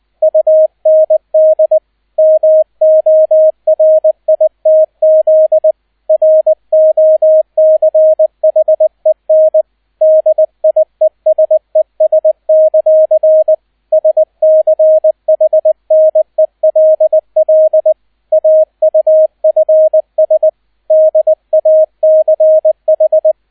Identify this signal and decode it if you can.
Morse code, heard as a keyed tone, about 20 wpm, 630 Hz, with radio noise. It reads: UND MORITZ ROCHEN DIESES; SCHNELL AUFS DACH
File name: 002 Zweiter Streich 20wpm